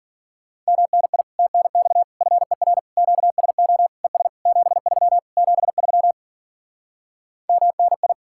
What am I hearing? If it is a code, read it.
Morse code, heard as a keyed tone, about 51 wpm, 700 Hz. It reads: MNI TNX FER QSO ES 73 73 MNI